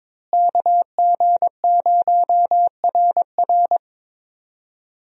Morse code, heard as a keyed tone, about 22 wpm, 700 Hz. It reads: KG0RR
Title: USA Call Signs - 5 Characters - Repeated 3x - 22wpm